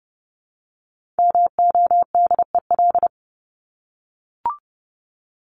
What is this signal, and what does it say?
Morse code, heard as a keyed tone, about 30 wpm, 700 Hz. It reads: MODEL
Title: Top 501-750 Words 30wpm